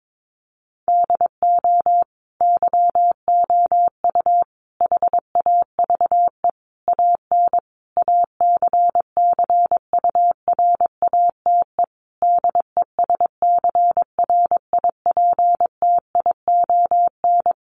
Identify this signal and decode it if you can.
Morse code, heard as a keyed tone, about 22 wpm, 700 Hz. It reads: DO YOU HAVE AN ACCURATE DESCRIPTION
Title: Sentences from Top 1300 Words 22wpm